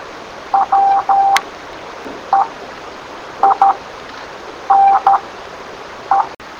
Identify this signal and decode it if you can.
Morse code, heard as a keyed tone, about 13 wpm, spaced as for 7 wpm, 780 Hz. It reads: WEINE